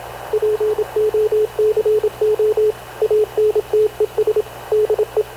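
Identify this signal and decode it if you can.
Morse code, heard as a keyed tone, about 27 wpm, 430 Hz, with radio noise. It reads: POCO ANTES DE
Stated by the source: Havana Radio CLA press (PX) broadcast, recorded by Radio Officer Paul Dumesnil aboard mv CNAV QUEST/CZDO.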